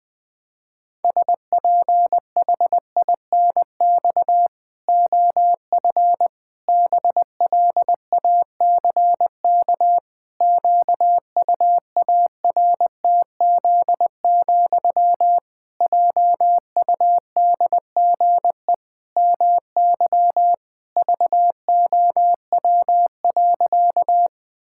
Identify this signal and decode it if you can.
Morse code, heard as a keyed tone, about 21 wpm, 700 Hz. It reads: SPHINX OF BLACK QUARTZ, JUDGE MY VOW.
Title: Pangram Challenge v2 - 20wpm